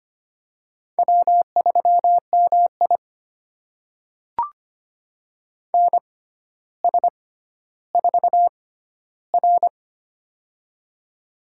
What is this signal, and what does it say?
Morse code, heard as a keyed tone, about 25 wpm, spaced as for 13 wpm, 700 Hz. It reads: W3MI NS4R
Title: USA Call Signs - 4 Characters - 6x Character Spacing - 25wpm